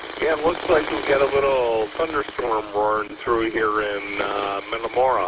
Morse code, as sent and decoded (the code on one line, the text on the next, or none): none